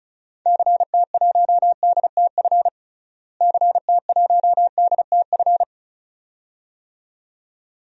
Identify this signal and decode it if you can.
Morse code, heard as a keyed tone, about 36 wpm, 700 Hz. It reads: CT1DTF CT1DTF
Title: DXCC-100 Call Signs - 6 Characters - Repeated 2x - 35wpm